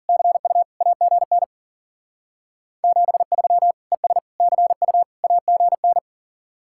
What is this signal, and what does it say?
Morse code, heard as a keyed tone, about 52 wpm, 700 Hz. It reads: CU AGN 73 ES CU AGN